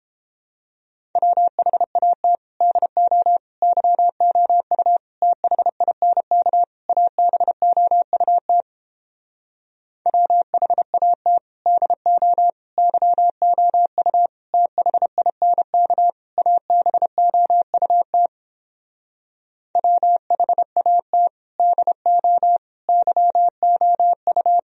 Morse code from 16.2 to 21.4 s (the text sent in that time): ABOUT WHAT